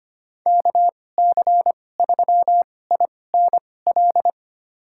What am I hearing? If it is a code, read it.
Morse code, heard as a keyed tone, about 25 wpm, spaced as for 18 wpm, 700 Hz. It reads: KC3INL
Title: USA Call Signs - 6 Characters - 2x Character Spacing - 25wpm